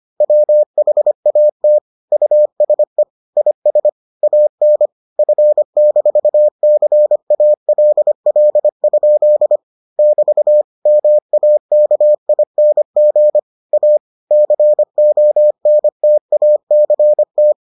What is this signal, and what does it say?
Morse code, heard as a keyed tone, about 25 wpm, 600 Hz. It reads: WHAT USE IS AN F-CALL? <BT> MAKING A CONTACT